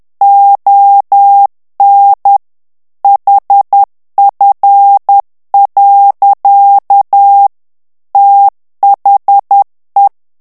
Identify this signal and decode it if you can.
Morse code, heard as a keyed tone, about 10 wpm, 800 Hz. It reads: ON HF. THE
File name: ANGLAIS texte clair 3 vitesse 10 mots minute